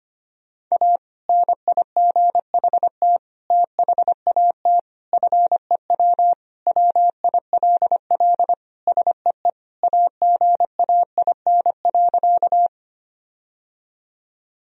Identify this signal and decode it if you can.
Morse code, heard as a keyed tone, about 25 wpm, 700 Hz. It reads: A NIGHT THAT FEW WILL SEE AGAIN.
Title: Sentences from Top 1000 Words - Review Part 1 - 25wpm